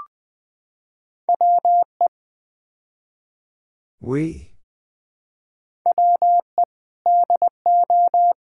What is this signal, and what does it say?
Morse code, heard as a keyed tone, about 20 wpm, 690 Hz. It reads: WE WE DO
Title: Build A Phrase - 5-Word Phrases - 20wpm